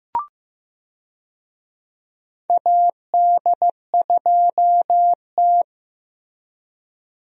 Morse code, heard as a keyed tone, about 15 wpm, 700 Hz. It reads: AD2T
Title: USA Call Signs - 4 Characters - Repeated 3x - 15wpm